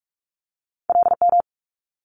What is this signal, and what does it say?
Morse code, heard as a keyed tone, about 46 wpm, 700 Hz. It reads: LM
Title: Multi-letter Phonemes Quick Fire 45wpm